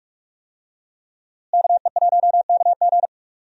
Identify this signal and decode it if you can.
Morse code, heard as a keyed tone, about 46 wpm, 700 Hz. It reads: KE1KG
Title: USA Call Signs - 5 Characters - Repeated 2x - 45wpm